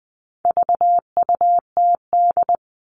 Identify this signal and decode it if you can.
Morse code, heard as a keyed tone, about 20 wpm, 700 Hz. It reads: VUTD
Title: All Cut Numbers - 4-Digit 20wpm